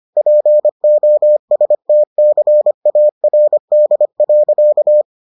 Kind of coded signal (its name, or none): Morse code